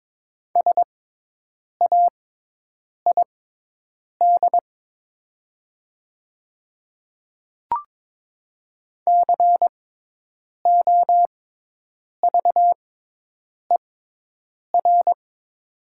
Morse code, heard as a keyed tone, about 22 wpm, spaced as for 8 wpm, 700 Hz. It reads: SAID COVER
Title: 6x Character Spacing - Top 300 Words - 22wpm